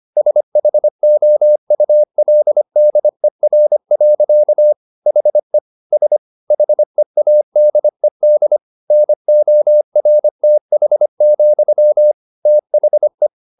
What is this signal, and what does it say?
Morse code, heard as a keyed tone, about 25 wpm, 600 Hz. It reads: SHOULDER. HE S HEADED NORTH, THE
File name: the old man and the sea part 1 25wpm